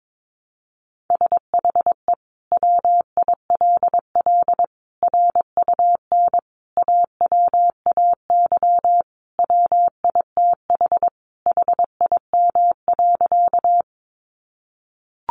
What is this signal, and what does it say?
Morse code, heard as a keyed tone, about 22 wpm, 700 Hz. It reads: SHE WILL RUN AWAY WITH HIM.
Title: Sentences from Top 600 Words 22wpm